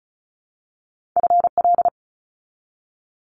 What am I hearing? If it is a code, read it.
Morse code, heard as a keyed tone, about 36 wpm, 700 Hz. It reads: FL
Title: Multi-letter Phonemes 35wpm